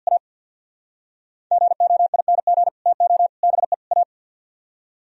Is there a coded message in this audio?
Yes: Morse code from 0.1 to 4.0 s, heard as a keyed tone, about 64 wpm, 700 Hz. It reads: A GOING TO BE A